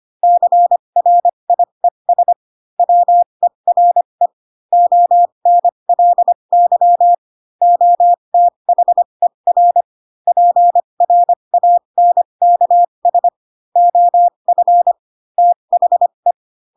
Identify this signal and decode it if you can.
Morse code, heard as a keyed tone, about 25 wpm, spaced as for 22 wpm, 700 Hz. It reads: CRIES WERE ONLY OTHER PRANKS OF THE
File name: Pinocchio / Pinocchio ch7 18 25 ik1hge